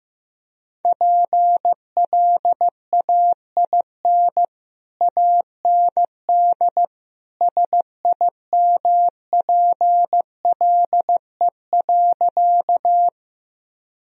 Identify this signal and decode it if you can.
Morse code, heard as a keyed tone, about 16 wpm, 700 Hz. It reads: PLAIN AND SIMPLE.
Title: Sentences from Top 1000 Words - Review Part 2 - 15wpm